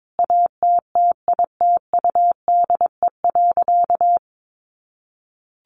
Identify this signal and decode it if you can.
Morse code, heard as a keyed tone, about 22 wpm, 700 Hz. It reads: ATTITUDE.
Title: Sentences from Top 1300 Words 22wpm